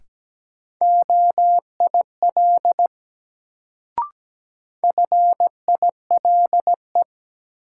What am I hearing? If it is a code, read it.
Morse code, heard as a keyed tone, about 17 wpm, 700 Hz. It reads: OIL FILE
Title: Sentences from Top 1000 Words - Review Part 4 - New Word Drill - 17wpm